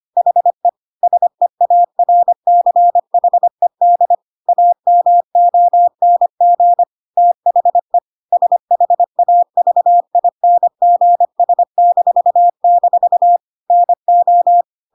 Morse code, heard as a keyed tone, about 25 wpm, 700 Hz. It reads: HE SEARCHED AMONG THE SHAVINGS-- NO
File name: Pinocchio / Pinocchio ch1 25 25 ik1hge